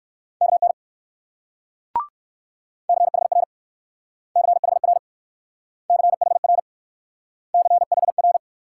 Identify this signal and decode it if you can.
Morse code, heard as a keyed tone, about 60 wpm, 700 Hz. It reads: KN CHR CHR CHR CHR